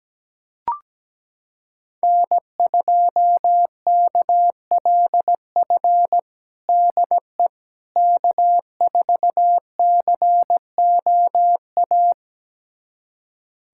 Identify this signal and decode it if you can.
Morse code, heard as a keyed tone, about 17 wpm, 700 Hz. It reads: N2KLF DE K4COA
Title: USA Callsign DE Callsign - 5 Characters - 17wpm